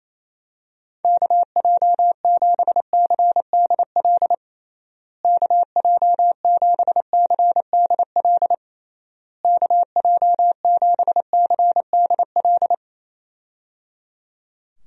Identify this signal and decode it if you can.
Morse code, heard as a keyed tone, about 28 wpm, 700 Hz. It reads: KJ7CDL KJ7CDL KJ7CDL
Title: USA Call Signs - 6 Characters - Repeated 3x - 28wpm